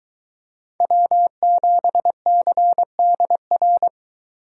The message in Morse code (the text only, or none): W7CDR